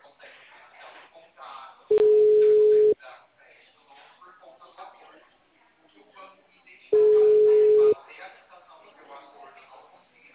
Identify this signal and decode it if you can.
Ringback tone (Europe, 425 Hz), with speech.
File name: out- -2001- - - .53174